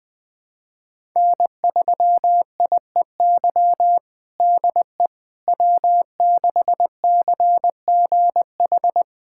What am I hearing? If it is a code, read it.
Morse code, heard as a keyed tone, about 20 wpm, 700 Hz. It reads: N3IEY DE W6CGH
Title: USA Callsign DE Callsign - 5 Characters - 20wpm